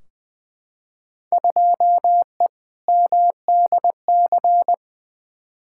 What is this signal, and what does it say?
Morse code, heard as a keyed tone, about 20 wpm, 700 Hz. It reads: 2E MDC
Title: Field Day Exchanges - 20wpm